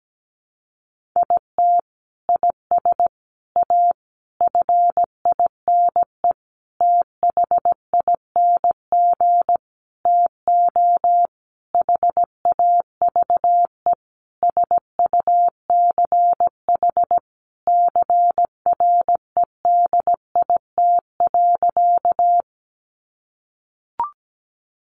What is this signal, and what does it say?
Morse code, heard as a keyed tone, about 17 wpm, 700 Hz. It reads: IT IS A FINE THING TO HAVE SUCH CREDIT.
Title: Sentences from Top 1000 Words 17wpm